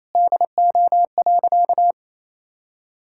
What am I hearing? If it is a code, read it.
Morse code, heard as a keyed tone, about 28 wpm, 700 Hz. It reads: DO.
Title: Sentences from Top 1000 Words - Review Part 6 - 28wpm